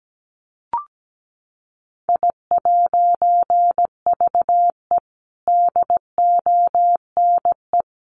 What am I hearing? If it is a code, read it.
Morse code, heard as a keyed tone, about 17 wpm, 700 Hz. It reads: I'VE DONE
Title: A Sound Of Thunder 17wpm